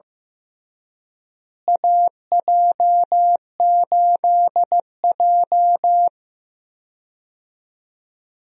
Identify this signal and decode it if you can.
Morse code, heard as a keyed tone, about 15 wpm, 700 Hz. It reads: AJ8J
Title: USA Call Signs - 4 Characters - Repeated 3x - 15wpm